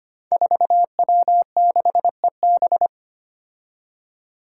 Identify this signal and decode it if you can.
Morse code, heard as a keyed tone, about 25 wpm, 700 Hz. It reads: 4W6EB
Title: Five Challenging Letter-Number Combinations 25wpm